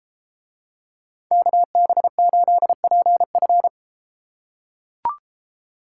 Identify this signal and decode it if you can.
Morse code, heard as a keyed tone, about 33 wpm, 700 Hz. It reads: KB8PF